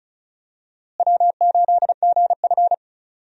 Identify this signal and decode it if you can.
Morse code, heard as a keyed tone, about 36 wpm, 700 Hz. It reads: W8GF
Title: USA Call Signs - 4 Characters - Repeated 2x - 35wpm